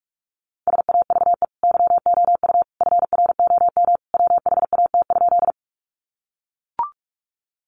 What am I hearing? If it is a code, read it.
Morse code, heard as a keyed tone, about 44 wpm, 700 Hz. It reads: SAVE YOU FROM WHAT?
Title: Sentences from Top 1100 Words 45wpm